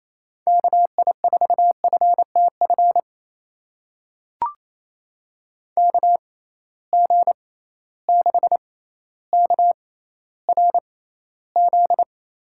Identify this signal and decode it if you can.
Morse code, heard as a keyed tone, about 28 wpm, spaced as for 14 wpm, 700 Hz. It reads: KI4FTF KG6KRZ
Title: USA Call Signs - 6 Characters - 6x Character Spacing - 28wpm